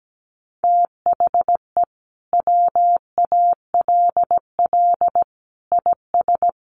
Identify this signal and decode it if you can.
Morse code, heard as a keyed tone, about 17 wpm, 700 Hz. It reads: THE WALL IS
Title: Sentences from Top 1400 Words 17wpm